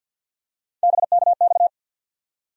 Morse code, heard as a keyed tone, about 52 wpm, 700 Hz. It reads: BKX
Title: USA Call Sign Postfixes - 3 Letters 50wpm